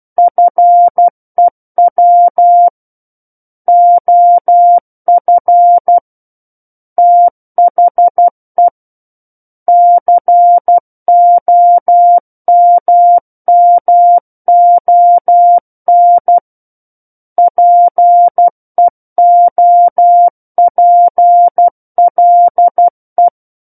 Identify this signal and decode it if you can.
Morse code, heard as a keyed tone, about 12 wpm, 700 Hz. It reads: FEW OF THE COMMON PEOPLE
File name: Book One / 03-Chapter 3 - 12 WPM